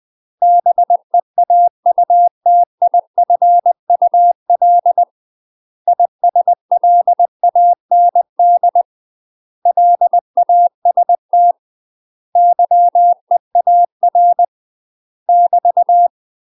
Morse code, heard as a keyed tone, about 20 wpm, 700 Hz. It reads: BEAUTIFUL ISLAND LAST YEAR <BT>
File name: UK MASTER QSOs TRAINING / 20WPM / UK MASTER QSOs 0071